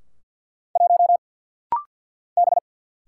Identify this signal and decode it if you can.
Morse code, heard as a keyed tone, about 50 wpm, 700 Hz. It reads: 1 B
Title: Single Letter-Number - Rapid-Fire 50wpm